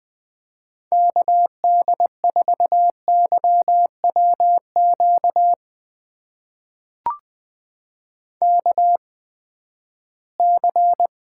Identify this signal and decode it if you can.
Morse code, heard as a keyed tone, about 19 wpm, spaced as for 14 wpm, 700 Hz. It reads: KD4YWQ KC
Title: USA Call Signs - 6 Characters - 8x Character Spacing - 20wpm